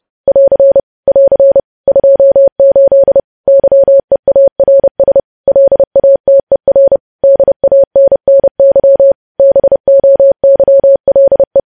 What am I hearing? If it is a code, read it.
Morse code, heard as a keyed tone, about 30 wpm, 550 Hz. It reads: <AR> <AR> 28 YEARS LATER DANNY BOYLE
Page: news headlines in morse code at 30 wpm